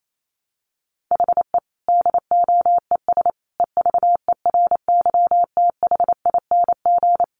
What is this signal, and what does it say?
Morse code, heard as a keyed tone, about 28 wpm, 700 Hz. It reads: HE DOES EVERYTHING
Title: Sentences from Top 900 Words 28wpm